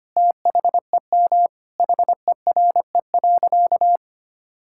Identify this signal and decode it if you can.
Morse code, heard as a keyed tone, about 25 wpm, 700 Hz. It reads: THEM HERE.
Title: Sentences from Top 1000 Words - Review Part 5 - 25wpm